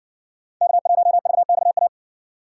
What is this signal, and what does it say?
Morse code, heard as a keyed tone, about 60 wpm, 700 Hz. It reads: K1VXA